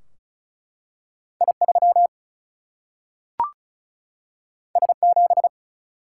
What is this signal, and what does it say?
Morse code, heard as a keyed tone, about 36 wpm, 700 Hz. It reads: I3 S7
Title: Single-Letter Single-Number 35wpm